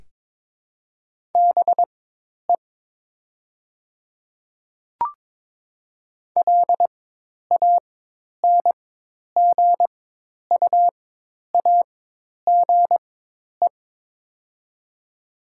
Morse code, heard as a keyed tone, about 22 wpm, spaced as for 10 wpm, 700 Hz. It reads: BE LANGUAGE
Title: 4x Character Spacing - Top 500 Words - 22wpm